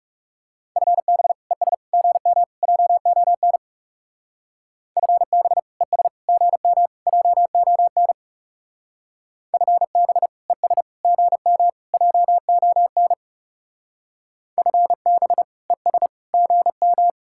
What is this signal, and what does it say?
Morse code, heard as a keyed tone, about 36 wpm, 700 Hz. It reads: FB ES GM JON FB ES GM JON FB ES GM JON FB ES GM
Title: Speed-Racing Common QSO Phrases Encore - 30wpm